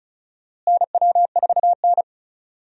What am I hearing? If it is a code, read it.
Morse code, heard as a keyed tone, about 34 wpm, 700 Hz. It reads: NW4N
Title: USA Call Signs - 4 Characters - Repeated 2x - 35wpm